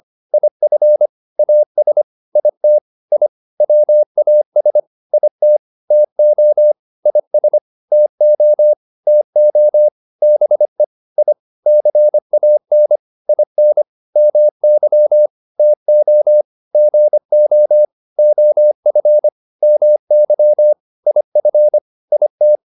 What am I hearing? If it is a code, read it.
Morse code, heard as a keyed tone, about 25 wpm, 600 Hz. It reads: IF AS IT I WAS IT TO IS TO TO BE I CAN IN MY TO GO OF MY IF IT